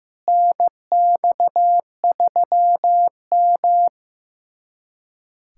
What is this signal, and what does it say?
Morse code, heard as a keyed tone, about 15 wpm, 700 Hz. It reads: NX3M
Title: USA Callsign DE Callsign - 4 Characters - 15wpm